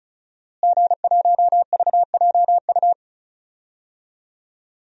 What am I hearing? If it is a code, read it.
Morse code, heard as a keyed tone, about 34 wpm, 700 Hz. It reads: G1VJU
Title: DXCC-100 Call Signs - 5 Characters - Repeated 3x - 35wpm